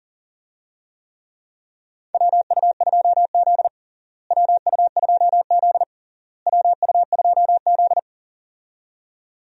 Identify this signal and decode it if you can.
Morse code, heard as a keyed tone, about 40 wpm, 700 Hz. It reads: WU2Z WU2Z WU2Z
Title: USA Call Signs - 4 Characters - Repeated 3x - 40wpm